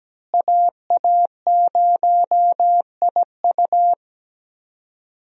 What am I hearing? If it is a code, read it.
Morse code, heard as a keyed tone, about 17 wpm, 700 Hz. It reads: AA0IU
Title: USA Call Signs - 5 Characters - Repeated 2x - 17wpm